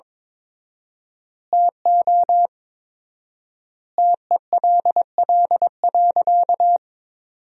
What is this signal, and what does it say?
Morse code, heard as a keyed tone, about 22 wpm, 700 Hz. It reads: TO TELL.
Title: 4x Word-Spacing - Sentences from Top 300 Words - 22wpm